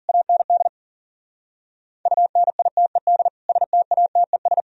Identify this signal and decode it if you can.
Morse code, heard as a keyed tone, about 48 wpm, 700 Hz. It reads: AND UNITED STATES
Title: Speed-Racing - Sets of 3 Words 40wpm